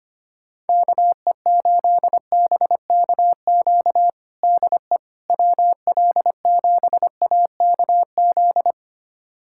Morse code, heard as a keyed tone, about 25 wpm, 700 Hz. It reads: KE8BKQ DE WL7AKZ
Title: USA Callsign DE Callsign - 6 Characters - 25wpm